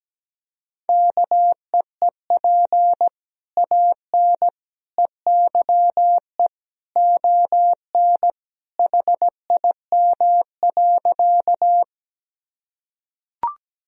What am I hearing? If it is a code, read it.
Morse code, heard as a keyed tone, about 17 wpm, 700 Hz. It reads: KEEP AN EYE ON HIM.
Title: Sentences from Top 1000 Words - Review Part 1 - 17wpm